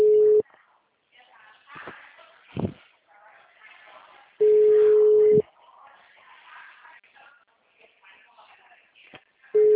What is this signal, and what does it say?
Ringback tone (Europe, 425 Hz), with speech.